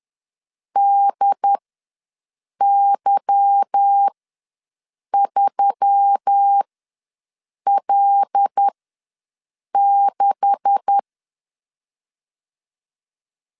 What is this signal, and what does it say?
Morse code, heard as a keyed tone, about 10 wpm, spaced as for 6 wpm, 800 Hz. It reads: DY3L6
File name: LEZIONI / 10 SIGNS / 10A SLASH ws 10wpm 4121